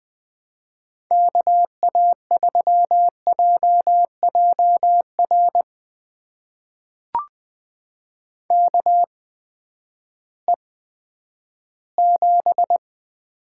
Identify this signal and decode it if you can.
Morse code, heard as a keyed tone, about 20 wpm, spaced as for 11 wpm, 700 Hz. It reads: KA3JJR KE7
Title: USA Call Signs - 6 Characters - 8x Character Spacing - 20wpm